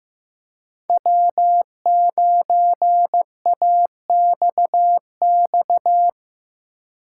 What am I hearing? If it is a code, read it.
Morse code, heard as a keyed tone, about 16 wpm, 700 Hz. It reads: W9AXX
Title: Speed Racing - USA Call Signs 15wpm